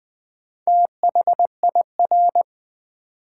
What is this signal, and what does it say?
Morse code, heard as a keyed tone, about 20 wpm, 700 Hz. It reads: THIR
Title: Build A Word - 2 Letters At A Time - 8-Letter Words - 20wpm